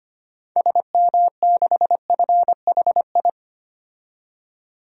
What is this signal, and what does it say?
Morse code, heard as a keyed tone, about 25 wpm, 700 Hz. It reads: SM6FHI
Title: DXCC-100 Call Signs - 6 Characters - 25wpm